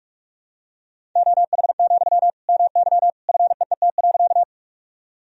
Morse code, heard as a keyed tone, about 44 wpm, 700 Hz. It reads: OH, MY FEET.
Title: Sentences from Top 1000 Words - Review Part 1 - 45wpm